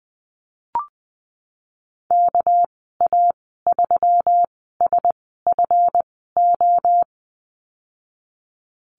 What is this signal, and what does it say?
Morse code, heard as a keyed tone, about 20 wpm, spaced as for 14 wpm, 700 Hz. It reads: KA3SFO
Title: 207 Lesson 33g - Letter j - Call Signs 20wpm